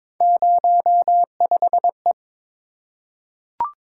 Morse code, heard as a keyed tone, about 22 wpm, 700 Hz. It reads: 05E
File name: practice / 22wpm / Three Letter-Number Combinations - 22wpm